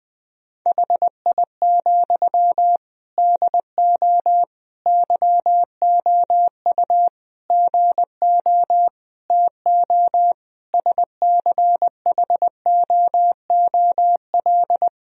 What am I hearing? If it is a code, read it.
Morse code, heard as a keyed tone, about 21 wpm, 700 Hz. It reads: HI, DO YOU GO TO SCHOOL
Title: Sentences from Top 1000 Words - Review Part 3 - 20wpm